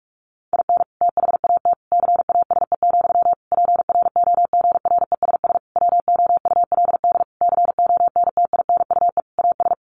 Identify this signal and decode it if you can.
Morse code, heard as a keyed tone, about 44 wpm, 700 Hz. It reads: IN THAT CASE, PROGRESS WOULD CONTINUE AS
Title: Sentences from Top 1400 Words 45wpm